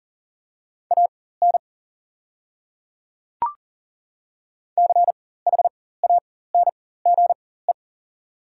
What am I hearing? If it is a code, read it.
Morse code, heard as a keyed tone, about 40 wpm, spaced as for 19 wpm, 700 Hz. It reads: AN CHANGE
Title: 4x Character Spacing - Top 200 Words - 40wpm